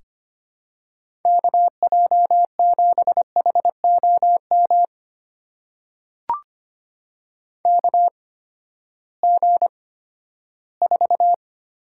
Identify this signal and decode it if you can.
Morse code, heard as a keyed tone, about 25 wpm, spaced as for 14 wpm, 700 Hz. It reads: KJ7HOM KG4
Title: USA Call Signs - 6 Characters - 8x Character Spacing - 25wpm